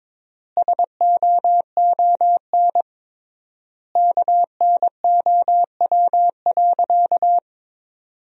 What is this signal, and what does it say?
Morse code, heard as a keyed tone, about 22 wpm, 700 Hz. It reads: SOON KNOW.
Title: 3x Word-Spacing - Sentences from Top 400 Words - 22wpm